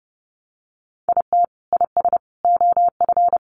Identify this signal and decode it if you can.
Morse code, heard as a keyed tone, about 30 wpm, 700 Hz. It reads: IT IS OF
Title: Sentences from Top 1200 Words 30wpm